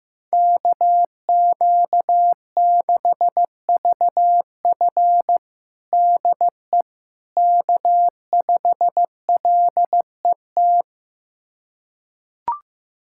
Morse code, heard as a keyed tone, about 15 wpm, 700 Hz. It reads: KQ6VF DE K5LET
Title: USA Callsign DE Callsign - 5 Characters - 15wpm